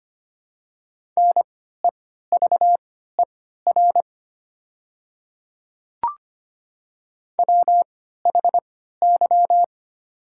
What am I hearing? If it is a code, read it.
Morse code, heard as a keyed tone, about 25 wpm, spaced as for 14 wpm, 700 Hz. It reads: NEVER WHY
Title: 3x Character Spacing - Top 300 Words - 25wpm